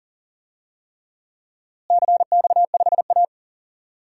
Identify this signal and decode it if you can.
Morse code, heard as a keyed tone, about 39 wpm, 700 Hz. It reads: CX5A
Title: DXCC-100 Call Signs - 4 Characters - Repeated 3x - 40wpm